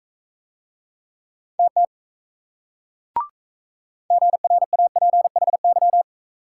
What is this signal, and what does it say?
Morse code, heard as a keyed tone, about 36 wpm, 700 Hz. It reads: TT GRAPHY